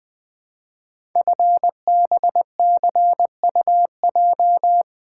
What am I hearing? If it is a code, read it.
Morse code, heard as a keyed tone, about 20 wpm, 700 Hz. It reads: FBCUJ
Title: Code Groups - 5 Letters - 20wpm